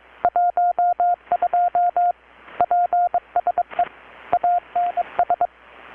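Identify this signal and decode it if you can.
Morse code, heard as a keyed tone, about 23 wpm, 720 Hz. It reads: 12 PSE ANS
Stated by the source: Morse from KPH's Night of Nights XX maritime radiotelegraphy event, received in Chiapas, Mexico about 3,800 km away on a Kenwood R-600 with a grounded 10 m long-wire antenna.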